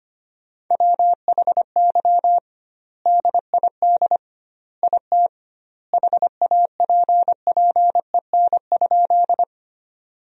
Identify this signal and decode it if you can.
Morse code, heard as a keyed tone, about 25 wpm, 700 Hz. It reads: WHY DID IT HAPPEN?
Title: 2x Word-Spacing - Sentences from Top 400 Words - 25wpm